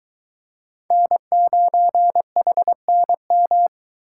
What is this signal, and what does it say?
Morse code, heard as a keyed tone, about 21 wpm, 700 Hz. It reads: N9HNM